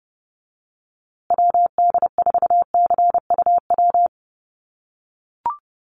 Morse code, heard as a keyed tone, about 30 wpm, 700 Hz. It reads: WD4CUW
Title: USA Call Signs Repeated 3x 30wpm